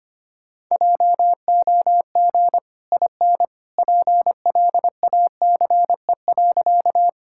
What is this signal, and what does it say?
Morse code, heard as a keyed tone, about 25 wpm, 700 Hz. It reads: JOG IN PLACE.
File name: Easy Sentences with Hints - 25wpm